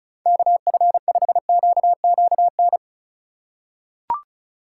Morse code, heard as a keyed tone, about 34 wpm, 700 Hz. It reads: KF5QQN
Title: USA Call Signs - 6 Characters - 2x Character Spacing - 35wpm